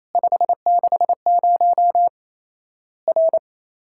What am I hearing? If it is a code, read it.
Morse code, heard as a keyed tone, about 28 wpm, 700 Hz. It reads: 560 R
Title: Worked All Europe DX Contest - QTC Exchange - 28wpm